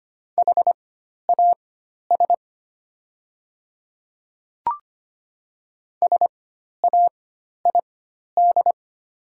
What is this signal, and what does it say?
Morse code, heard as a keyed tone, about 25 wpm, spaced as for 12 wpm, 700 Hz. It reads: HAS SAID
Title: 4x Character Spacing - Top 100 Words - 25wpm